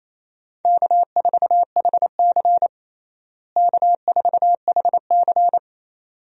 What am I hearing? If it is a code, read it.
Morse code, heard as a keyed tone, about 28 wpm, 700 Hz. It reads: K4HC K4HC
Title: USA Call Signs - 4 Characters - Repeated 2x - 28wpm